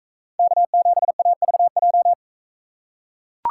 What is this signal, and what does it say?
Morse code, heard as a keyed tone, about 42 wpm, 700 Hz. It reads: K7AVJ